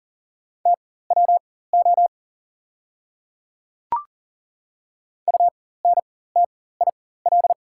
Morse code, heard as a keyed tone, about 39 wpm, spaced as for 19 wpm, 690 Hz. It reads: TWO UNTIL
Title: 4x Character Spacing - Top 300 Words - 40wpm